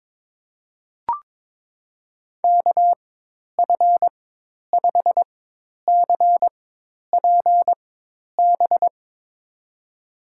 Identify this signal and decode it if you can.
Morse code, heard as a keyed tone, about 22 wpm, spaced as for 10 wpm, 700 Hz. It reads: KF5CPB
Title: USA Call Signs - 6 Characters - 4x Character Spacing - 22wpm